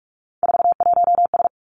Morse code, heard as a keyed tone, about 46 wpm, 700 Hz. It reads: 41S
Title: Three Challenging Letter-Number Combinations 45wpm